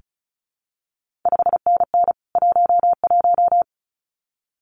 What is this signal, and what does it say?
Morse code, heard as a keyed tone, about 36 wpm, 700 Hz. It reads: 5NN 11
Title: CQ WW DX Contest Exchange Encore - 35wpm